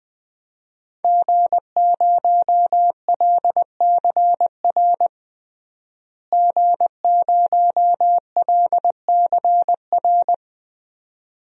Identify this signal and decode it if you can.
Morse code, heard as a keyed tone, about 21 wpm, 700 Hz. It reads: G0LCR G0LCR
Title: DXCC-100 Call Signs - 5 Characters - Repeated 2x - 20wpm